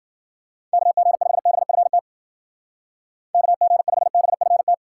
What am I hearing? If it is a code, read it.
Morse code, heard as a keyed tone, about 50 wpm, 700 Hz. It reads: KG5BFT KG5BFT
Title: Speed Racing - USA 2x3 Call Signs 40wpm